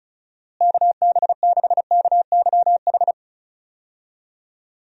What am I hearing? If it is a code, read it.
Morse code, heard as a keyed tone, about 34 wpm, 700 Hz. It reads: KB6KYH
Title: USA Call Signs - 6 Characters - Repeated 2x - 35wpm